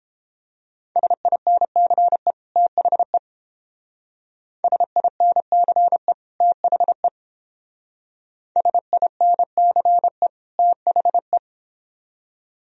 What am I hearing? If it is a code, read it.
Morse code, heard as a keyed tone, about 29 wpm, 700 Hz. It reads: SINCE THE SINCE THE SINCE THE